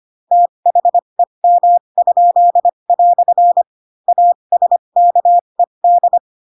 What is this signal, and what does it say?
Morse code, heard as a keyed tone, about 25 wpm, spaced as for 22 wpm, 700 Hz. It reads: THEM?" ASKED
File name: Pinocchio / Pinocchio ch7 18 25 ik1hge